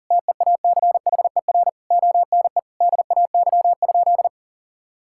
Morse code, heard as a keyed tone, about 40 wpm, 700 Hz. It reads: TEACHER ONE DAY?
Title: Sentences from Top 1000 Words - Review Part 6 - 40wpm